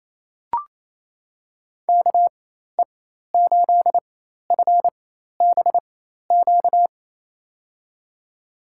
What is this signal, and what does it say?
Morse code, heard as a keyed tone, about 28 wpm, spaced as for 13 wpm, 700 Hz. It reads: KE8FBQ
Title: USA Call Signs - 6 Characters - 4x Character Spacing - 28wpm